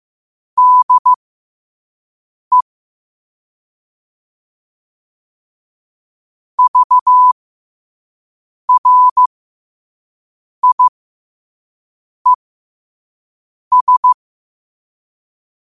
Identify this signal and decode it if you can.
Morse code, heard as a keyed tone, about 15 wpm, spaced as for 5 wpm, 1000 Hz. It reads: DE VRIES
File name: qotdfiles / 5wpm